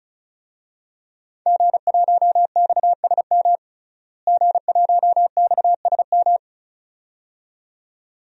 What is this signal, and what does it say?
Morse code, heard as a keyed tone, about 34 wpm, 700 Hz. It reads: G1XSM G1XSM
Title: DXCC-100 Call Signs - 5 Characters - Repeated 2x - 35wpm